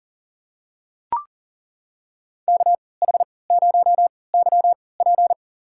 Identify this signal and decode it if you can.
Morse code, heard as a keyed tone, about 40 wpm, spaced as for 23 wpm, 700 Hz. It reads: KH0YP
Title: USA Call Signs - 3x Character Spacing - 40wpm